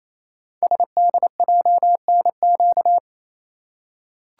Morse code, heard as a keyed tone, about 29 wpm, 700 Hz. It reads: SDJNQ
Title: Code Groups - 5 Letter:Numbers - 28wpm